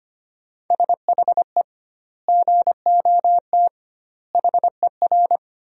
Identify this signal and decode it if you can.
Morse code, heard as a keyed tone, about 25 wpm, 700 Hz. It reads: SHE GOT HER
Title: 2x Word-Spacing - Sentences from Top 500 Words - 25wpm